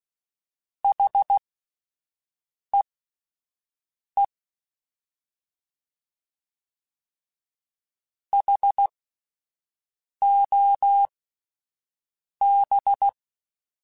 Morse code, heard as a keyed tone, about 16 wpm, spaced as for 5 wpm, 800 Hz. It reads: HEE HOB
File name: Alphabetical / Alphabetical mp3s / Hs @5